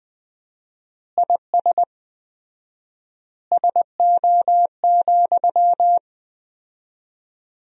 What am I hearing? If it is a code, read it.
Morse code, heard as a keyed tone, about 21 wpm, 700 Hz. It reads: IS SO,
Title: 4x Word-Spacing - Sentences from Top 200 Words - 20wpm